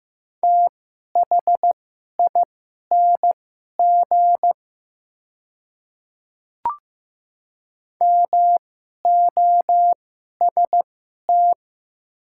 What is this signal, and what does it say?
Morse code, heard as a keyed tone, about 15 wpm, spaced as for 11 wpm, 700 Hz. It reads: THING MOST
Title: 2x Character Spacing - Top 100 Words - 15wpm